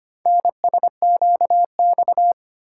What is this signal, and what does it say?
Morse code, heard as a keyed tone, about 25 wpm, 700 Hz. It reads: NSQX
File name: practice / 25wpm / Four Letter-Number Combinations - 25wpm